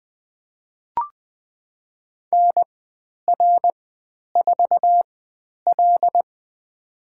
Morse code, heard as a keyed tone, about 20 wpm, spaced as for 10 wpm, 700 Hz. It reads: NR4L
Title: 095 Lesson 16g - Letter c - Call Signs 20wpm